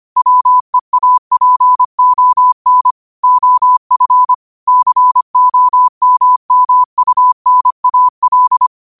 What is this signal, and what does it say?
Morse code, heard as a keyed tone, about 25 wpm, 1000 Hz. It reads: WEAPON OF COMMUNAL
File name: qotdfiles / 25wpm